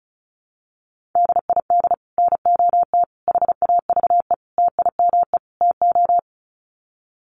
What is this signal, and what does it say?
Morse code, heard as a keyed tone, about 34 wpm, 700 Hz. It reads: DID NOT HAVE TIME TO
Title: Sets of 5 Words Encore 35wpm